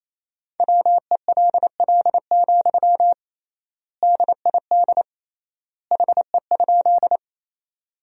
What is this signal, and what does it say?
Morse code, heard as a keyed tone, about 27 wpm, 700 Hz. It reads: WELL, DID HE?
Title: 3x Word-Spacing - Sentences from Top 200 Words - 28wpm